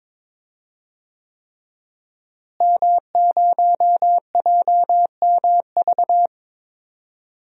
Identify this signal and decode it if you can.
Morse code, heard as a keyed tone, about 22 wpm, 700 Hz. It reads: M0JMV
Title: DXCC-100 Call Signs - 5 Characters - Repeated 3x - 22wpm